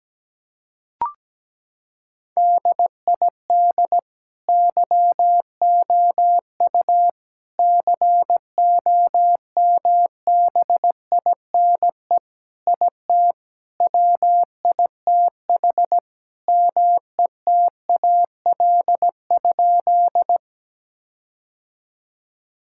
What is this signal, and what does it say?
Morse code, heard as a keyed tone, about 17 wpm, 700 Hz. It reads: DID YOU COMBINE IT WITH METAL?
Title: Sentences from Top 1600 Words 17wpm